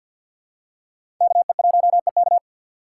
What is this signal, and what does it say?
Morse code, heard as a keyed tone, about 47 wpm, 690 Hz. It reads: KE1EK
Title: USA Call Signs - 5 Characters - 3x Character Spacing - 50wpm